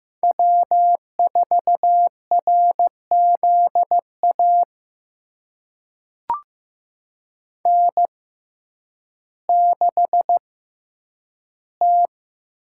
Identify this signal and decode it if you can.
Morse code, heard as a keyed tone, about 15 wpm, spaced as for 9 wpm, 700 Hz. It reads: W4RZA N6T
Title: USA Call Signs - 5 Characters - 6x Character Spacing - 15wpm